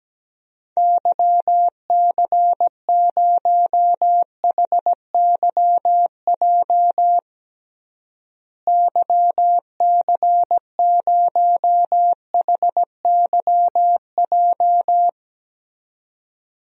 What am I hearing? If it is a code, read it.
Morse code, heard as a keyed tone, about 17 wpm, 700 Hz. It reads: YC0HYJ YC0HYJ
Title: DXCC-100 Call Signs - 6 Characters - Repeated 2x - 17wpm